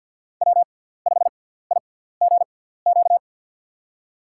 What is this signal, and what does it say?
Morse code, heard as a keyed tone, about 47 wpm, spaced as for 17 wpm, 700 Hz. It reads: W5IGQ
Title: USA Call Signs - 5 Characters - 6x Character Spacing - 50wpm